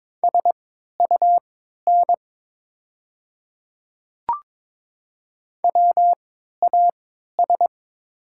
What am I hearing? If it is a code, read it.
Morse code, heard as a keyed tone, about 22 wpm, spaced as for 12 wpm, 690 Hz. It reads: SUN WAS
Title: 3x Character Spacing - Top 300 Words - 22wpm